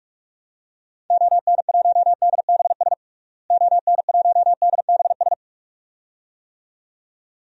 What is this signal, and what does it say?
Morse code, heard as a keyed tone, about 46 wpm, 700 Hz. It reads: ON1DBS ON1DBS
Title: DXCC-100 Call Signs - 6 Characters - Repeated 2x - 45wpm